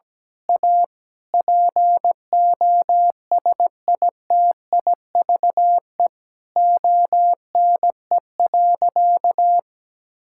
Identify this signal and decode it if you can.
Morse code, heard as a keyed tone, about 17 wpm, 700 Hz. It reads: A POSITIVE ONE.
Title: Sentences from Top 1000 Words - Review Part 6 - 17wpm